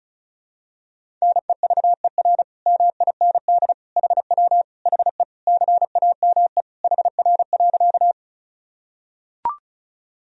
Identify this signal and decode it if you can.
Morse code, heard as a keyed tone, about 36 wpm, 700 Hz. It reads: NEVER MIND HW HE CAME HR.
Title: Sentences from Top 500 Words and Abbreviations 35wpm